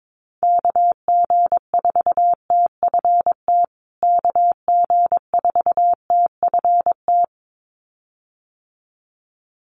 Morse code, heard as a keyed tone, about 22 wpm, 700 Hz. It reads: KG4TFT KG4TFT
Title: USA Call Signs Repeated 2x 22wpm